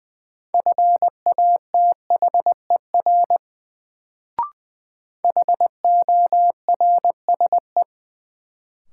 Morse code, heard as a keyed tone, about 21 wpm, 700 Hz. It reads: FATHER HORSE
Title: Sentences from Top 300 Words - New Word Drill - 20wpm